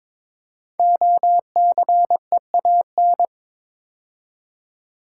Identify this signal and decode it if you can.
Morse code, heard as a keyed tone, about 22 wpm, 700 Hz. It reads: OCEAN
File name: 5 Letter Words - First 3 Letters Given - 22wpm